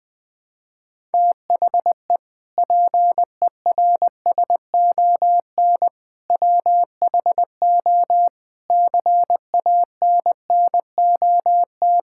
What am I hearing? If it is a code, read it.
Morse code, heard as a keyed tone, about 20 wpm, 700 Hz. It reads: THE PERSON WHO CANNOT
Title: Sentences with 8-Word Phrases - 20wpm